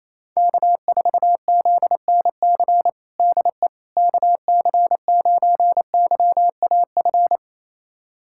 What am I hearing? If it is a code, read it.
Morse code, heard as a keyed tone, about 28 wpm, 700 Hz. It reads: K4ZNC DE KC9YAF
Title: USA Callsign DE Callsign 28wpm